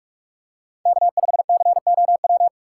Morse code, heard as a keyed tone, about 44 wpm, 700 Hz. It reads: K5COW
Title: USA Call Signs - 5 Characters - 3x Character Spacing - 45wpm